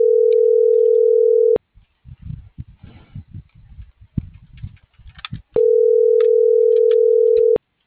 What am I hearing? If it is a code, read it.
Ringback tone (North America), 440 + 480 Hz.